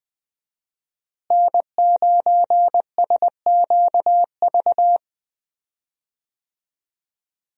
Morse code, heard as a keyed tone, about 20 wpm, 700 Hz. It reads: N9SQV
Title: USA Call Signs - 5 Characters - Repeated 2x - 20wpm